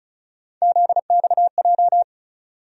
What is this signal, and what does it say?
Morse code, heard as a keyed tone, about 34 wpm, 700 Hz. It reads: ZXJ
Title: USA Call Sign Postfixes - 3 Letters 35wpm